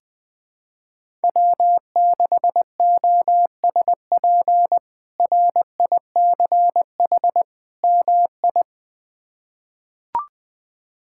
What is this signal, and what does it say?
Morse code, heard as a keyed tone, about 20 wpm, 700 Hz. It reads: W6OSP RICH MI
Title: K1USN SST QSO Practice - 20wpm